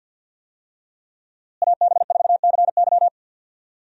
Morse code, heard as a keyed tone, about 52 wpm, 700 Hz. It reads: AB4CY
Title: USA Call Signs - 5 Characters - Repeated 2x - 50wpm